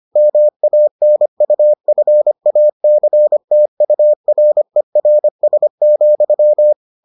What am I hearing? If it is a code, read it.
Morse code, heard as a keyed tone, about 25 wpm, 600 Hz. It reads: MANUFACTURERS,